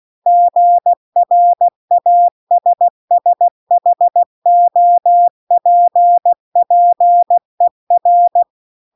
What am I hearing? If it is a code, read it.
Morse code, heard as a keyed tone, about 16 wpm, 700 Hz. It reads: GRASSHOPPER
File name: CW-The-Big-Twoheaded-River-2-16WPM